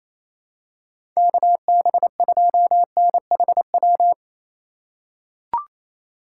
Morse code, heard as a keyed tone, about 28 wpm, 700 Hz. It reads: KB2NHW
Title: USA Call Signs - 6 Characters - 3x Character Spacing - 28wpm